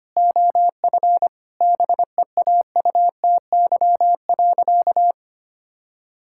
Morse code, heard as a keyed tone, about 25 wpm, 700 Hz. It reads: OF BEAUTY.
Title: Sentences from Top 1000 Words - Review Part 6 - 25wpm